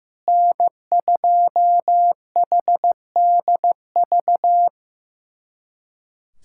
Morse code, heard as a keyed tone, about 15 wpm, 700 Hz. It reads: N2HDV
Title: SOTA QSO Practice 15wpm